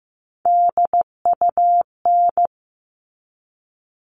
Morse code, heard as a keyed tone, about 15 wpm, 700 Hz. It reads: DUN
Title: All Cut Numbers - 3-Digit 15wpm